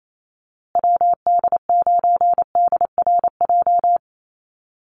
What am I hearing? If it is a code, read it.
Morse code, heard as a keyed tone, about 28 wpm, 700 Hz. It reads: WD9DRJ
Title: USA Call Signs Repeated 3x 28wpm